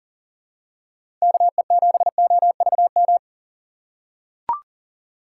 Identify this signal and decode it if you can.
Morse code, heard as a keyed tone, about 40 wpm, 700 Hz. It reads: KE7OVM
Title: USA Call Signs - 6 Characters - 2x Character Spacing - 40wpm